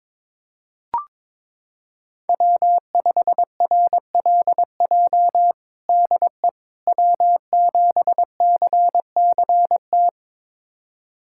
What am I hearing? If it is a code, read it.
Morse code, heard as a keyed tone, about 22 wpm, 700 Hz. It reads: W5RLJ DE W7CCT
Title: USA Callsign DE Callsign - 5 Characters - 22wpm